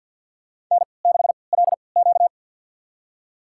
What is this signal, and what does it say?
Morse code, heard as a keyed tone, about 52 wpm, spaced as for 29 wpm, 700 Hz. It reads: N6LQ
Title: USA Call Signs - 4 Characters - 3x Character Spacing - 50wpm